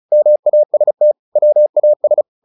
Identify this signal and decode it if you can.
Morse code, heard as a keyed tone, about 34 wpm, 600 Hz. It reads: MAST WAS
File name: the old man and the sea / the old man and the sea 1 35wpm